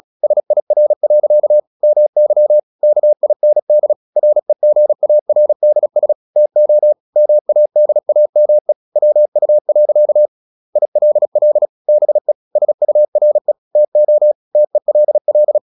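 Morse code, heard as a keyed tone, about 36 wpm, 600 Hz. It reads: SIR. MY KIND REGARDS TO MADAME WU. ILL BE SURE TO TELL
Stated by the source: Morse practice at 36 wpm, text of Ken Follett's novel Never, from CW med Gnister.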